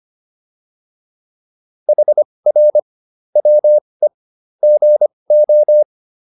Morse code, heard as a keyed tone, about 25 wpm, spaced as for 20 wpm, 600 Hz. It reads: HR WE GO